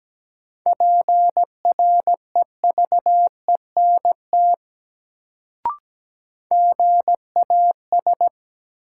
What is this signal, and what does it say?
Morse code, heard as a keyed tone, about 17 wpm, 700 Hz. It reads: PREVENT GAS
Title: Sentences from Top 1100 Words - New Word Drill Alt - 17wpm